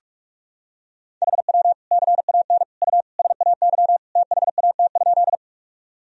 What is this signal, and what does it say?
Morse code, heard as a keyed tone, about 46 wpm, 700 Hz. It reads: HW CAN U SAY THAT?
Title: Sentences from Top 200 Words and Abbreviations 45wpm